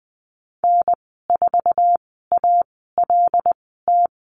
Morse code, heard as a keyed tone, about 20 wpm, spaced as for 15 wpm, 700 Hz. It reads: N4ALT
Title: 140 Lesson 23g - Letter f - Call Signs 20wpm